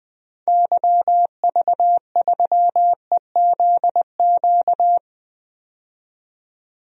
Morse code, heard as a keyed tone, about 21 wpm, 700 Hz. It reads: YV3EZQ
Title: DXCC-100 Call Signs - 6 Characters - 20wpm